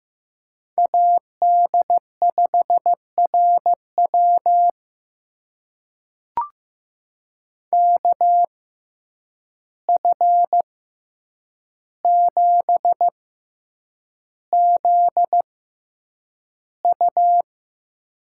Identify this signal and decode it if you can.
Morse code, heard as a keyed tone, about 15 wpm, spaced as for 8 wpm, 700 Hz. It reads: AD5RW KF7ZU
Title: USA Call Signs - 5 Characters - 6x Character Spacing - 15wpm